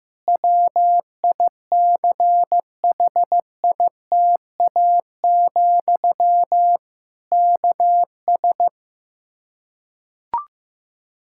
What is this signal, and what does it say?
Morse code, heard as a keyed tone, about 15 wpm, 700 Hz. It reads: WICHITA, KS
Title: K1USN SST QSO Practice 15wpm